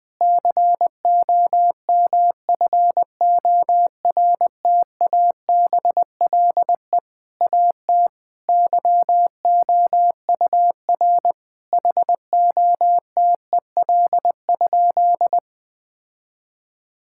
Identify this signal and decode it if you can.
Morse code, heard as a keyed tone, about 20 wpm, 700 Hz. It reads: COMFORTABLE AT YOUR HOTEL?
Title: Sentences from Top 1600 Words 20wpm